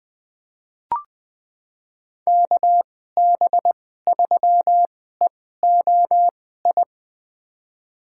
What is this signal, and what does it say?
Morse code, heard as a keyed tone, about 21 wpm, spaced as for 15 wpm, 700 Hz. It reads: KB3EOI
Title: USA Call Signs - 6 Characters - 2x Character Spacing - 20wpm